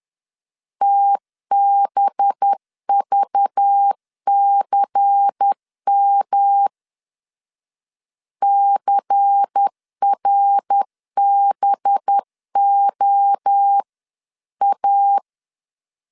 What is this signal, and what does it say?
Morse code, heard as a keyed tone, about 10 wpm, 800 Hz. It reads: TBVCM CRBOA
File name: LEZIONI / 6 PKX0 / 6 PK 10wpm 2436